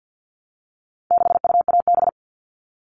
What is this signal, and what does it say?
Morse code, heard as a keyed tone, about 50 wpm, 700 Hz. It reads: 6UAB
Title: All Cut Numbers - 4-Digit 50wpm